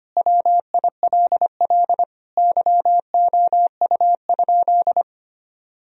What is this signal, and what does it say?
Morse code, heard as a keyed tone, about 25 wpm, 700 Hz. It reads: WILL YOU?
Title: Sentences from Top 1000 Words - Review Part 4 - 25wpm